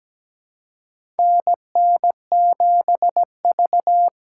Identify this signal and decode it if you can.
Morse code, heard as a keyed tone, about 17 wpm, 700 Hz. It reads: NN7V
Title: USA Call Signs - 4 Characters - 3x Character Spacing - 17wpm